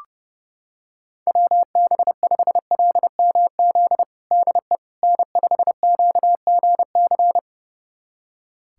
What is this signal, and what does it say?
Morse code, heard as a keyed tone, about 30 wpm, 700 Hz. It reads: WB5LMZ DE N5QGC
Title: USA Callsign DE Callsign 30wpm